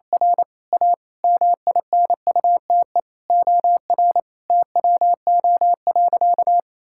Morse code, heard as a keyed tone, about 28 wpm, 700 Hz. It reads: R A MINUTE OR TWO.
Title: Sentences from Top 1600 Words 28wpm